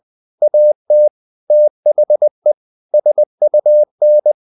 Morse code, heard as a keyed tone, about 20 wpm, 600 Hz. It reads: AT THE SUN
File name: the old man and the sea part 2 20wpm